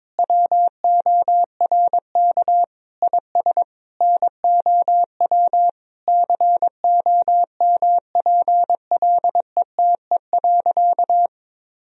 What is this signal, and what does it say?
Morse code, heard as a keyed tone, about 22 wpm, 700 Hz. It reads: WORK IS NOW COMPLETE.
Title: Sentences from Top 1000 Words - Review Part 2 - 22wpm